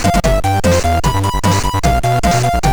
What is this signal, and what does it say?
There is no code.